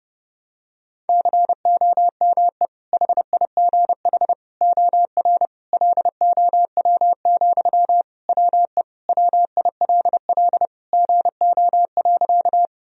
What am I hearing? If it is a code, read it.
Morse code, heard as a keyed tone, about 32 wpm, 700 Hz. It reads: COME HIGH OR LOW, WE WILL GO.
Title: Sentences from Top 1000 Words - Review Part 1 - 30wpm